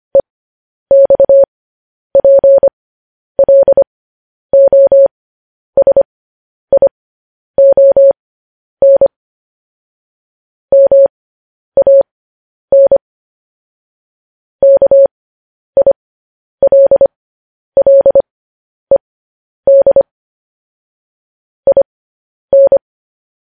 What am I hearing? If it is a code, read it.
Morse code, heard as a keyed tone, about 25 wpm, spaced as for 10 wpm, 550 Hz. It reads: EXPLOSION MAN KILLED IN
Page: News Headlines in Morse Code at 10 WPM